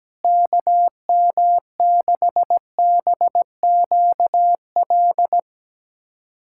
Morse code, heard as a keyed tone, about 17 wpm, 700 Hz. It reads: KM6BQL
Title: USA Callsign DE Callsign - 6 Characters - 17wpm